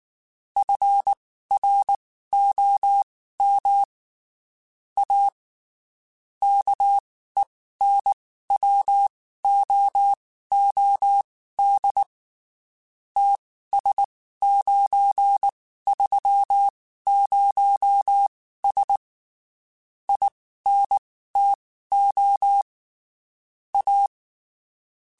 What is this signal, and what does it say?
Morse code, heard as a keyed tone, about 19 wpm, spaced as for 14 wpm, 800 Hz. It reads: FROM A KENWOOD TS930S INTO A
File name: (spaced) / QSOs Spaced-mp3s / kc7haa @18 sp